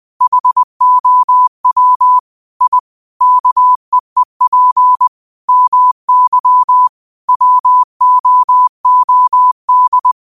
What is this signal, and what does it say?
Morse code, heard as a keyed tone, about 20 wpm, 1000 Hz. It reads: HOW I KEEP MY WOOD